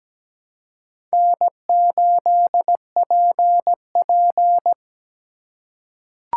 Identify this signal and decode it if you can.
Morse code, heard as a keyed tone, about 17 wpm, 700 Hz. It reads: N8PP
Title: USA Call Signs - 4 Characters - 3x Character Spacing - 17wpm